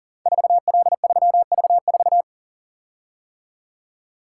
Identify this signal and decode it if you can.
Morse code, heard as a keyed tone, about 42 wpm, 690 Hz. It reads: 4L3V4
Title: Code Groups - 5 Letter:Numbers - 40wpm